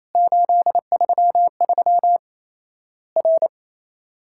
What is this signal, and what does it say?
Morse code, heard as a keyed tone, about 28 wpm, 700 Hz. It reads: 833 R
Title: Worked All Europe DX Contest - QTC Exchange - 28wpm